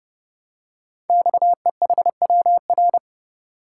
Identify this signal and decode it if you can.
Morse code, heard as a keyed tone, about 30 wpm, 700 Hz. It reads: XEHWR
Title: Code Groups - 5 Letters - 30wpm